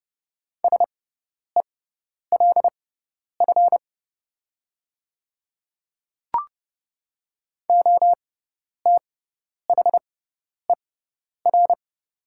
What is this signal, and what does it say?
Morse code, heard as a keyed tone, about 30 wpm, spaced as for 10 wpm, 700 Hz. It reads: SELF OTHER
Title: 6x Character Spacing - Top 200 Words - 30wpm